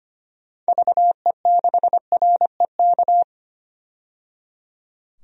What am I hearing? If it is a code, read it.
Morse code, heard as a keyed tone, about 25 wpm, 690 Hz. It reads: VE6REK
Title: DXCC-100 Call Signs - 6 Characters - 25wpm